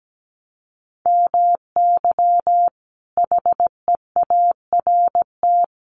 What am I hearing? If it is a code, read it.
Morse code, heard as a keyed tone, about 17 wpm, 700 Hz. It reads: MY HEART
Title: Sentences from Top 1200 Words 17wpm